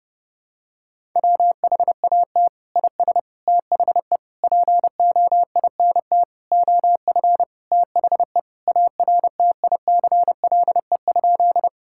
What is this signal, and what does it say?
Morse code, heard as a keyed tone, about 30 wpm, 700 Hz. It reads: WHAT IS THE POINT OF THE ARTICLE?
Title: Sentences from Top 1000 Words - Review Part 5 - 30wpm